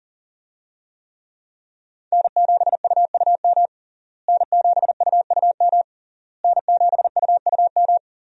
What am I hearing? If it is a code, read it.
Morse code, heard as a keyed tone, about 37 wpm, 700 Hz. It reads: N7UUM N7UUM N7UUM
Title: USA Call Signs - 5 Characters - Repeated 3x - 40wpm